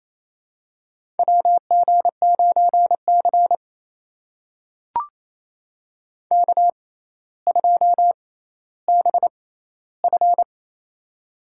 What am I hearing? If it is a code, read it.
Morse code, heard as a keyed tone, about 28 wpm, spaced as for 14 wpm, 700 Hz. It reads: WG9C K2BF
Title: USA Call Signs - 4 Characters - 6x Character Spacing - 28wpm